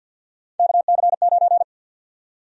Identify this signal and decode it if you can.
Morse code, heard as a keyed tone, about 52 wpm, 700 Hz. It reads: KC9
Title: USA Call Sign Prefixes - 50wpm